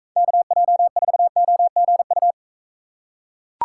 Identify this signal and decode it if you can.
Morse code, heard as a keyed tone, about 43 wpm, 700 Hz. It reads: KJ4OGU